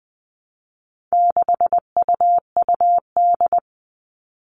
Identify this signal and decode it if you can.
Morse code, heard as a keyed tone, about 20 wpm, 700 Hz. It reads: 6UUD
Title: All Cut Numbers - 4-Digit 20wpm